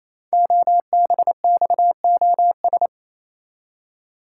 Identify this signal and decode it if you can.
Morse code, heard as a keyed tone, about 27 wpm, 700 Hz. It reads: OBXOS
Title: Code Groups - 5 Letters - 28wpm